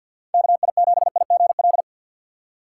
Morse code, heard as a keyed tone, about 49 wpm, 700 Hz. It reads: KI7IGL
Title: USA Call Signs - 6 Characters - 4x Character Spacing - 50wpm